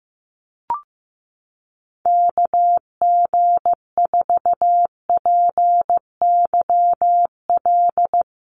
Morse code, heard as a keyed tone, about 15 wpm, 700 Hz. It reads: KG4PYL
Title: USA Call Signs Repeated 2x 15wpm